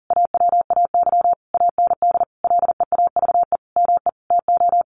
Morse code, heard as a keyed tone, about 37 wpm, 700 Hz. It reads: AWAY AND LEAVE ME TO
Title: Sentences from Top 1300 Words 40wpm